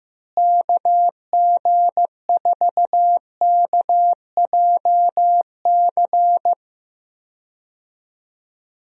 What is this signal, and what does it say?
Morse code, heard as a keyed tone, about 15 wpm, 700 Hz. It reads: KG4KJC
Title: USA Call Signs - 6 Characters - Repeated 2x - 15wpm